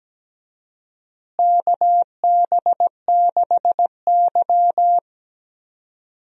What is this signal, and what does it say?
Morse code, heard as a keyed tone, about 17 wpm, 700 Hz. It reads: KB6Y
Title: USA Call Signs - 4 Characters - Repeated 3x - 17wpm